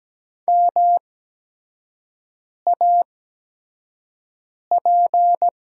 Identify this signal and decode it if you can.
Morse code, heard as a keyed tone, about 17 wpm, spaced as for 5 wpm, 700 Hz. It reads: MAP
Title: 8x Character Spacing - Top 500 Words - 17wpm